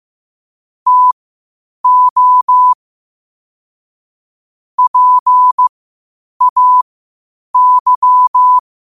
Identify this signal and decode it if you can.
Morse code, heard as a keyed tone, about 15 wpm, spaced as for 8 wpm, 1000 Hz. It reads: TO PAY
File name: qotdfiles / 8wpm